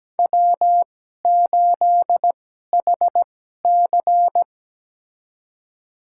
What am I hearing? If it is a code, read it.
Morse code, heard as a keyed tone, about 17 wpm, spaced as for 12 wpm, 700 Hz. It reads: W8HC
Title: USA Call Signs - 2x Character Spacing - 17wpm